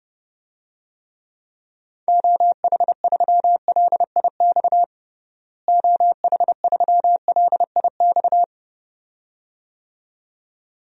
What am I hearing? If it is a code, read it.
Morse code, heard as a keyed tone, about 30 wpm, 700 Hz. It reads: OH3LIX OH3LIX
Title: DXCC-100 Call Signs - 6 Characters - Repeated 2x - 30wpm